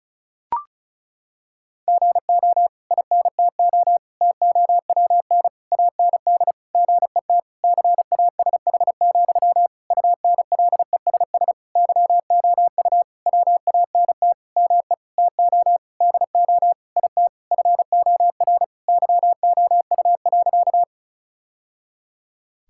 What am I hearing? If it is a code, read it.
Morse code, heard as a keyed tone, about 34 wpm, 700 Hz. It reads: GO INTO TOWN AND GET CASH, UNLESS YOU WANT ME TO DO IT FOR YOU.
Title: Sentences from Top 1000 Words - Review Part 6 - 35wpm